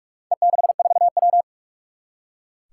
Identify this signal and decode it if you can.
Morse code, heard as a keyed tone, about 44 wpm, 700 Hz. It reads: E64W
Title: Four Challenging Letter-Number Combinations 45wpm